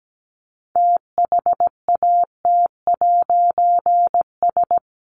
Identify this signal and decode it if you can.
Morse code, heard as a keyed tone, about 17 wpm, 700 Hz. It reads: THAT'S
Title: A Sound Of Thunder 17wpm